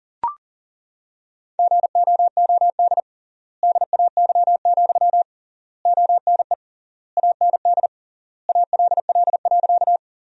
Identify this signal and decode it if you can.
Morse code, heard as a keyed tone, about 37 wpm, 700 Hz. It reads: GOOD DAY, ONE AND ALL.
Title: 3x Word-Spacing - Sentences from Top 200 Words - 40wpm